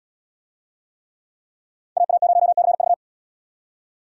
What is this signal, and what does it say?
Morse code, heard as a keyed tone, about 78 wpm, 700 Hz. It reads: AA0CV